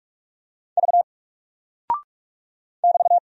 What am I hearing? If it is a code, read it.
Morse code, heard as a keyed tone, about 44 wpm, 700 Hz. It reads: V <BT>
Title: Single Character (Letter:Number:Punctuation:Prosign) 45wpm